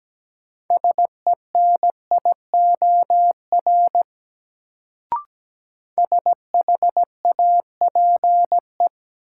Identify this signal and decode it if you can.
Morse code, heard as a keyed tone, about 17 wpm, 700 Hz. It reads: SENIOR SHAPE
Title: Sets of 5 Words Encore - New Word Drill - 17wpm